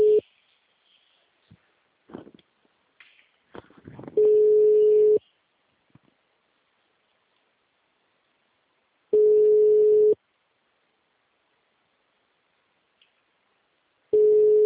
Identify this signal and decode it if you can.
Ringback tone (Europe, 425 Hz).
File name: out- -2041- - - .44728